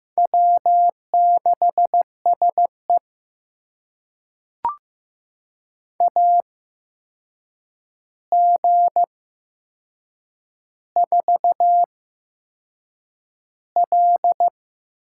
Morse code, heard as a keyed tone, about 15 wpm, spaced as for 6 wpm, 700 Hz. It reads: W6SE AG4L
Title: USA Call Signs - 4 Characters - 8x Character Spacing - 15wpm